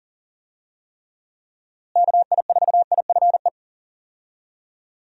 Morse code, heard as a keyed tone, about 40 wpm, 700 Hz. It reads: KI4IFE
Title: USA Call Signs - 6 Characters - Repeated 3x - 40wpm